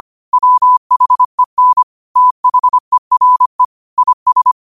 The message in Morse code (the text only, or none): WHEN THERE IS